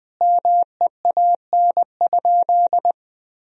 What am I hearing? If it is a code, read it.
Morse code, heard as a keyed tone, about 20 wpm, 700 Hz. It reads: MEAN?
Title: Sentences from Top 1000 Words - Review Part 4 - 20wpm